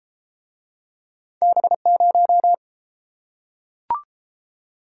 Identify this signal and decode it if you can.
Morse code, heard as a keyed tone, about 31 wpm, 700 Hz. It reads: B0